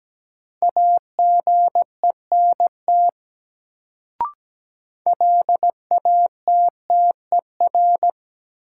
Morse code, heard as a keyed tone, about 17 wpm, 700 Hz. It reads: AGENT LATTER
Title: Sentences from Top 1600 Words - New Word Drill - 17wpm